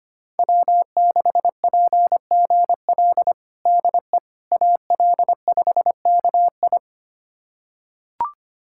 Morse code, heard as a keyed tone, about 25 wpm, 700 Hz. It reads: W6PGL DE AL5KI
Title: USA Callsign DE Callsign - 5 Characters - 25wpm